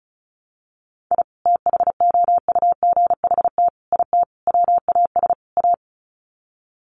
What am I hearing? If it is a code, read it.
Morse code, heard as a keyed tone, about 34 wpm, 700 Hz. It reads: I THOUGHT IT WAS A
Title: Sets of 5 Words 35wpm